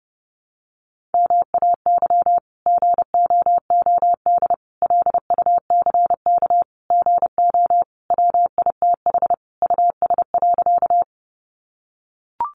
Morse code, heard as a keyed tone, about 30 wpm, 700 Hz. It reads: MAY GOOD LUCK GO WITH US.
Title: Sentences from Top 1500 Words 30wpm